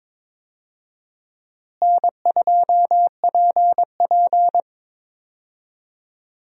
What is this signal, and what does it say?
Morse code, heard as a keyed tone, about 22 wpm, 700 Hz. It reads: N2PP
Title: USA Call Signs - 4 Characters - Repeated 2x - 22wpm